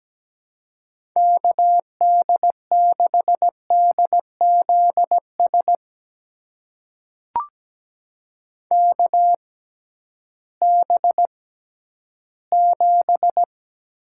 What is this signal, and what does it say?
Morse code, heard as a keyed tone, about 17 wpm, spaced as for 11 wpm, 700 Hz. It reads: KD6DZS KB7
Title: USA Call Signs - 6 Characters - 6x Character Spacing - 17wpm